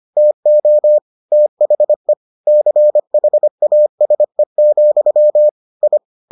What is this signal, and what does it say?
Morse code, heard as a keyed tone, about 25 wpm, 600 Hz. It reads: TO THE CHASE, I